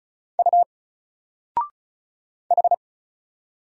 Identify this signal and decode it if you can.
Morse code, heard as a keyed tone, about 33 wpm, 700 Hz. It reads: U H
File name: Single-Letter-Number-35wpm